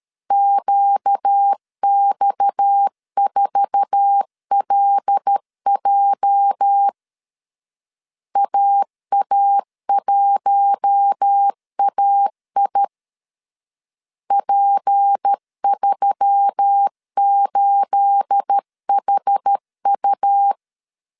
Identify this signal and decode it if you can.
Morse code, heard as a keyed tone, about 13 wpm, 800 Hz. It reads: QX4LJ AA1AI P38HU
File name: CW / DET / 1 9 1500C S C 12wpm 3811